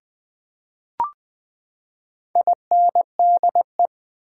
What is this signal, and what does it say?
Morse code, heard as a keyed tone, about 20 wpm, 700 Hz. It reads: INDE
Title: Build A Word - 4 Letters At A Time - 12-Letter Words - 20wpm